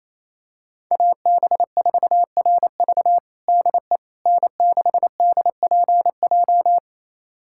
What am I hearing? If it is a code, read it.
Morse code, heard as a keyed tone, about 28 wpm, 690 Hz. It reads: AB4RV DE N6DPJ
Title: USA Callsign DE Callsign - 5 Characters - 28wpm